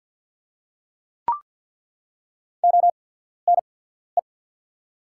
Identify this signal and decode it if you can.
Morse code, heard as a keyed tone, about 48 wpm, spaced as for 14 wpm, 700 Hz. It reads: ONE
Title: 8x Character Spacing - Top 100 Words - 50wpm